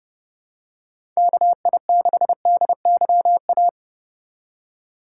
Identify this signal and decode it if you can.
Morse code, heard as a keyed tone, about 30 wpm, 700 Hz. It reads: KI6DYA
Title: USA Call Signs - 2x Character Spacing - 30wpm